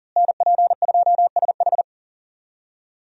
Morse code, heard as a keyed tone, about 40 wpm, 700 Hz. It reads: NP2SH
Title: USA Call Signs - 5 Characters - Repeated 2x - 40wpm